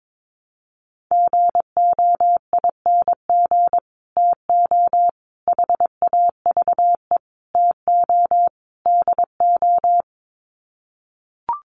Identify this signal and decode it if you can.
Morse code, heard as a keyed tone, about 22 wpm, 700 Hz. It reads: GOING TO HAVE TO DO
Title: Sets of 5 Words Encore 22wpm